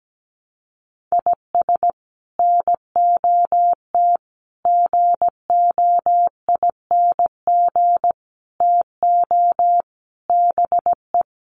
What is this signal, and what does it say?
Morse code, heard as a keyed tone, about 17 wpm, 700 Hz. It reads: IS NOT GOING TO BE
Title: Sets of 5 Words Encore2 17wpm